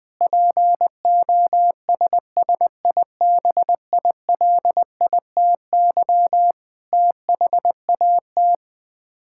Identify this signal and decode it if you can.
Morse code, heard as a keyed tone, about 20 wpm, 700 Hz. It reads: POSSIBILITY THAT
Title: Sets of 8 Words - 20wpm